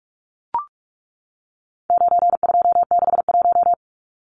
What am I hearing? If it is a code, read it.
Morse code, heard as a keyed tone, about 44 wpm, 700 Hz. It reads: 9261
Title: Numbers - 4-Digit 45wpm